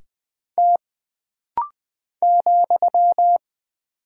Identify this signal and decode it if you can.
Morse code, heard as a keyed tone, about 20 wpm, 700 Hz. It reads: T ,
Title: Single Character (Letter:Number:Punctuation:Prosign) - Rapid-Fire - 20wpm